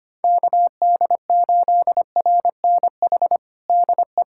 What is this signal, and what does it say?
Morse code, heard as a keyed tone, about 25 wpm, 700 Hz. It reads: KD8RNH DE
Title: USA Callsign DE Callsign 25wpm